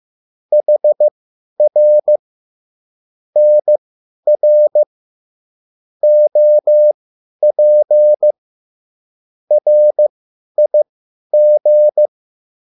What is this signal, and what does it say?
Morse code, heard as a keyed tone, about 15 wpm, spaced as for 10 wpm, 600 Hz. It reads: HR NR OP RIG